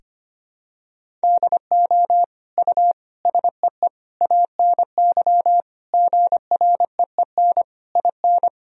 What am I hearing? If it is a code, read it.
Morse code, heard as a keyed tone, about 25 wpm, 700 Hz. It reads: DO U SEE ANY GREEN IN
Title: Sentences from Top 500 Words and Abbreviations 25wpm